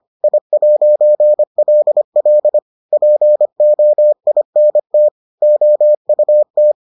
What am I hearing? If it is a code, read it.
Morse code, heard as a keyed tone, about 25 wpm, 600 Hz. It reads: I'LL POINT OUT